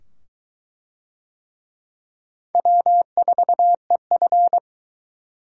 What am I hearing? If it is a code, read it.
Morse code, heard as a keyed tone, about 21 wpm, 700 Hz. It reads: W4EF